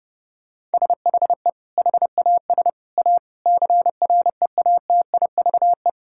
Morse code, heard as a keyed tone, about 30 wpm, 700 Hz. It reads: SHE HAS A CREATIVE
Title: Sentences from Top 1000 Words - Review Part 5 - 30wpm